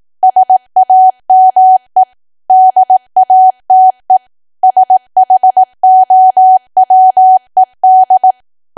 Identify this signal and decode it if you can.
Morse code, heard as a keyed tone, about 18 wpm, 750 Hz. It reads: SAME DATE SHOWED
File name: cwa / W1AW-Level2 / 18WPM